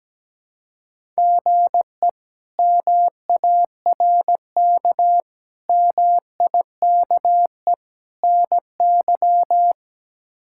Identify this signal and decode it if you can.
Morse code, heard as a keyed tone, about 17 wpm, 700 Hz. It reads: GE MARK MIKE NY
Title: K1USN SST QSO Practice - 17wpm